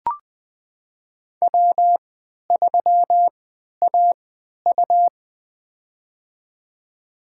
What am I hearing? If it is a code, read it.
Morse code, heard as a keyed tone, about 20 wpm, spaced as for 11 wpm, 700 Hz. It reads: W3AU
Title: USA Call Signs - 4 Characters - 3x Character Spacing - 20wpm